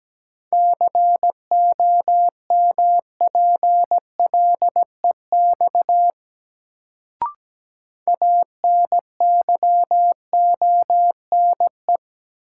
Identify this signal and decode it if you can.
Morse code, heard as a keyed tone, about 17 wpm, 700 Hz. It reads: COMPLEX ANYONE
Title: Sentences from Top 900 Words - New Word Drill - 17wpm